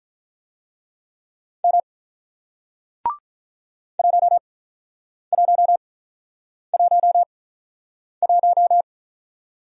Morse code, heard as a keyed tone, about 48 wpm, spaced as for 9 wpm, 700 Hz. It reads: M 1111